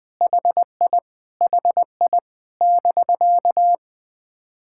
Morse code, heard as a keyed tone, about 20 wpm, 700 Hz. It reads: HI HI <BK>
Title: POTA QSO Practice - 20wpm